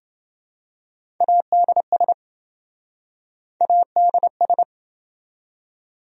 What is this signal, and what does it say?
Morse code, heard as a keyed tone, about 27 wpm, 700 Hz. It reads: ADS ADS